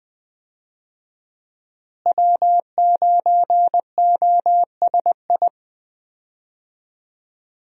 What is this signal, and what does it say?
Morse code, heard as a keyed tone, about 20 wpm, 700 Hz. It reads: W9OSI
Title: USA Call Signs - 5 Characters - Repeated 3x - 20wpm